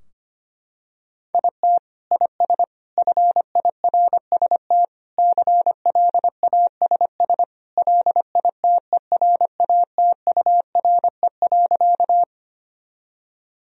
Morse code, heard as a keyed tone, about 25 wpm, 700 Hz. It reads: IT IS FIRST CLASS LITERATURE.
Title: Sentences from Top 1000 Words - Review Part 3 - 25wpm